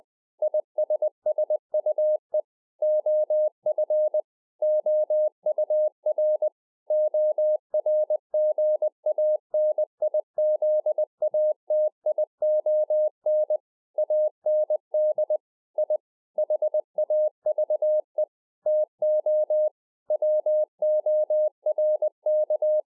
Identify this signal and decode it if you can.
Morse code, heard as a keyed tone, about 20 wpm, 610 Hz. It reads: ISSUE OF OUR ORGANIZATION AND I HAVE TO WORK
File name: 20WPM